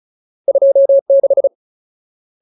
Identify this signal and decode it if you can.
Morse code, heard as a keyed tone, about 34 wpm, 550 Hz. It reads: 26
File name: Ultimate Word List / 35WPM Ultimate